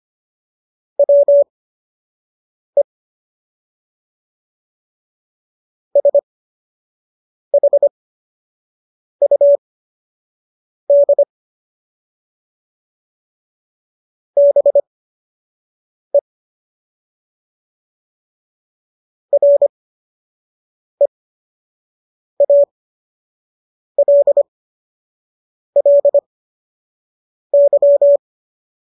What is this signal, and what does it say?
Morse code, heard as a keyed tone, about 25 wpm, spaced as for 6 wpm, 570 Hz. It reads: WE SHUD BE REALLY